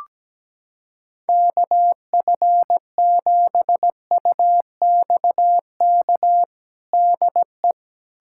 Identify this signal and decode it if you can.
Morse code, heard as a keyed tone, about 17 wpm, 700 Hz. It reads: KF7UXK DE
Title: USA Callsign DE Callsign 17wpm